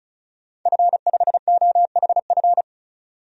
Morse code, heard as a keyed tone, about 36 wpm, 700 Hz. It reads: F5OHF
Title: DXCC-100 Call Signs - 5 Characters - 35wpm